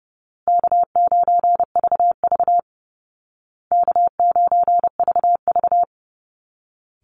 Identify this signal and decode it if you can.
Morse code, heard as a keyed tone, about 30 wpm, 700 Hz. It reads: K9VV K9VV
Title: Call Signs Repeated 2x - 30wpm